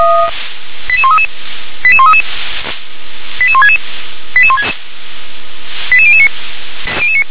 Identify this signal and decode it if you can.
Five-tone selective call (ZVEI), heard as quick runs of five tones, tones of about 70 ms. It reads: 88133 88133 88166 88009